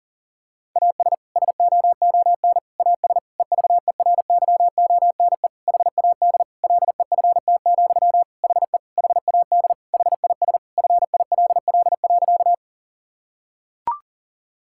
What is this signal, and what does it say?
Morse code, heard as a keyed tone, about 40 wpm, 700 Hz. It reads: AS SOON AS EVERYONE HAD LEFT, HE HAD HIS FILL.
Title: Sentences from Top 1000 Words - Review Part 5 - 40wpm